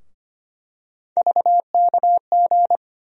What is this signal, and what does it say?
Morse code, heard as a keyed tone, about 25 wpm, 700 Hz. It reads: VKG
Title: USA Call Sign Postfixes - 3 Letters 25wpm